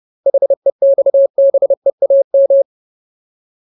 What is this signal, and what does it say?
Morse code, heard as a keyed tone, about 30 wpm, 550 Hz. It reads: HEXBEAM
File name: Ultimate Word List / 30WPM Ultimate